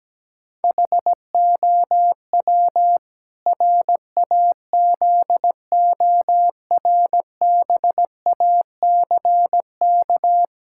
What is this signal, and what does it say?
Morse code, heard as a keyed tone, about 17 wpm, 700 Hz. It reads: HOW RAZORBACK
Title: Pangram Challenge v2 - 17wpm